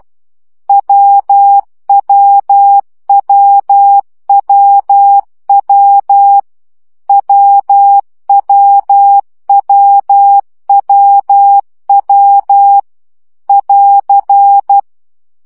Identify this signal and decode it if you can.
Morse code, heard as a keyed tone, about 12 wpm, 800 Hz. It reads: WWWWW WWWWW <AR>
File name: CW gyakorlas w 10 baud